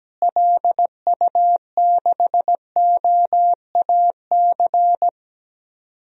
Morse code, heard as a keyed tone, about 17 wpm, 700 Hz. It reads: LU6OAC
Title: DXCC-100 Call Signs - 6 Characters - Repeated 2x - 17wpm